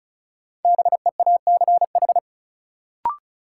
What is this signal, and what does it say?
Morse code, heard as a keyed tone, about 34 wpm, 700 Hz. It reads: BEACH
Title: Sentences from Top 1500 Words - New Word Drill Alt - 35wpm